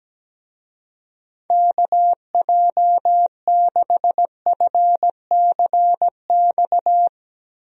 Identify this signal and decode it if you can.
Morse code, heard as a keyed tone, about 17 wpm, 700 Hz. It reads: KJ6FCX
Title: USA Call Signs - 6 Characters - Repeated 3x - 17wpm